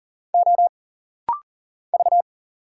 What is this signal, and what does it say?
Morse code, heard as a keyed tone, about 40 wpm, 700 Hz. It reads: O V
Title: Single Letter-Number - Rapid-Fire 40wpm